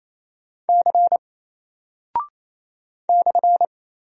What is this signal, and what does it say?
Morse code, heard as a keyed tone, about 27 wpm, 700 Hz. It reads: C /
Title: Single Character (Letter:Number:Punctuation:Prosign) 28wpm